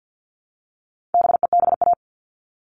Morse code, heard as a keyed tone, about 50 wpm, 700 Hz. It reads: BEBA
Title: All Cut Numbers - 4-Digit 50wpm